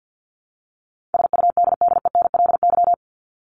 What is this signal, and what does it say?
Morse code, heard as a keyed tone, about 50 wpm, 700 Hz. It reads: SUDDENLY
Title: LY Suffix - Top 100 Words with Base Words 50wpm